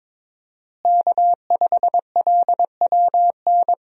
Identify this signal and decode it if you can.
Morse code, heard as a keyed tone, about 22 wpm, 700 Hz. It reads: K5LWN
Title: USA Call Signs - 5 Characters - Repeated 3x - 22wpm